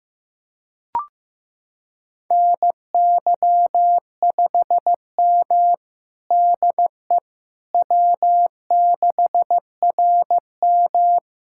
Morse code, heard as a keyed tone, about 16 wpm, 700 Hz. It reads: NY5M DE W6RM
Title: USA Callsign DE Callsign - 4 Characters - 15wpm